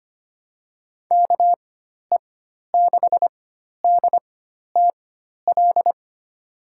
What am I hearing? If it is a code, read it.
Morse code, heard as a keyed tone, about 25 wpm, spaced as for 12 wpm, 700 Hz. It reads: KE6DTL
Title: USA Call Signs - 4x Character Spacing - 25wpm